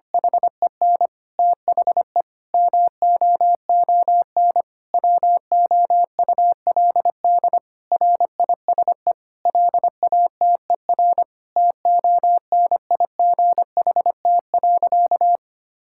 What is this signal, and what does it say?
Morse code, heard as a keyed tone, about 25 wpm, 700 Hz. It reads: HEN THE MOON WOULD RISE LATER TONIGHT.
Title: Sentences from Top 1000 Words - Review Part 6 - 25wpm